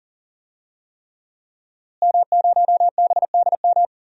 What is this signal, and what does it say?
Morse code, heard as a keyed tone, about 40 wpm, 700 Hz. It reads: M0BDM
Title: DXCC-100 Call Signs - 5 Characters - Repeated 3x - 40wpm